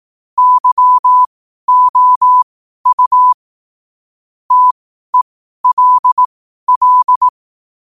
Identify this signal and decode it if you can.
Morse code, heard as a keyed tone, about 18 wpm, spaced as for 12 wpm, 1000 Hz. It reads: YOU TELL
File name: qotdfiles / 12wpm